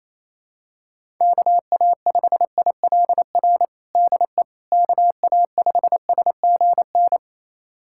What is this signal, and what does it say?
Morse code, heard as a keyed tone, about 28 wpm, 700 Hz. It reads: KA5ILR DE KA5SGN
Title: USA Callsign DE Callsign - 6 Characters - 28wpm